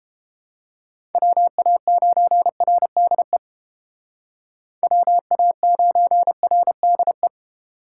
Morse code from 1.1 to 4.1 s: WA9RDE

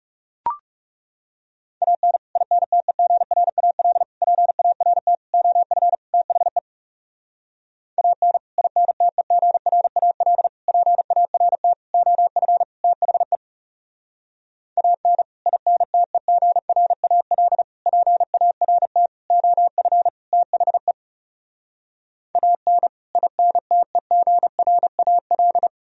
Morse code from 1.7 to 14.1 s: AN INTEGRAL PART OF THE AN INTEGRAL PART OF THE

